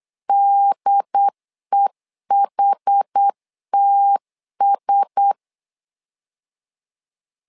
Morse code, heard as a keyed tone, about 8 wpm, 800 Hz. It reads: DEHTS
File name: LEZIONI / 5 LBVC / 5 L 8wpm 2831